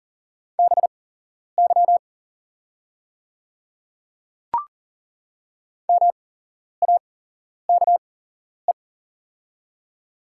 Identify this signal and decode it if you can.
Morse code, heard as a keyed tone, about 40 wpm, spaced as for 11 wpm, 700 Hz. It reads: BY MAKE
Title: 8x Character Spacing - Top 100 Words - 40wpm